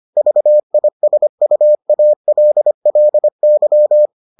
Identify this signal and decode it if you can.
Morse code, heard as a keyed tone, about 25 wpm, 600 Hz. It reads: VISUALLY